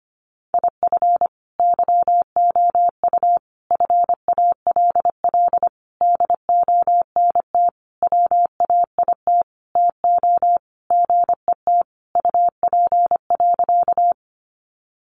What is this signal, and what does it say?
Morse code, heard as a keyed tone, about 25 wpm, 700 Hz. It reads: IF YOU FALL DONT WAIT TO GET UP.
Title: Sentences from Top 500 Words 25wpm